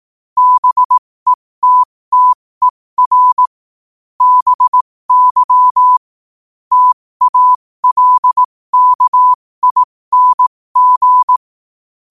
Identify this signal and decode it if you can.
Morse code, heard as a keyed tone, about 18 wpm, spaced as for 15 wpm, 1000 Hz. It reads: BETTER BY TALKING
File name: qotdfiles / 15wpm